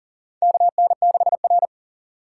Morse code, heard as a keyed tone, about 40 wpm, 700 Hz. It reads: KN6R
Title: USA Call Signs - 4 Characters - 6x Character Spacing - 40wpm